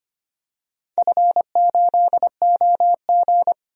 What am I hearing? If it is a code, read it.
Morse code, heard as a keyed tone, about 25 wpm, 700 Hz. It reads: F8OG
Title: DXCC-100 Call Signs - 4 Characters - 25wpm